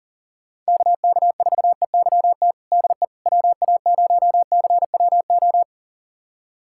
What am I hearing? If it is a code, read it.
Morse code, heard as a keyed tone, about 40 wpm, 700 Hz. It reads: KK4EYT DE WA0CWO
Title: USA Callsign DE Callsign - 6 Characters - 40wpm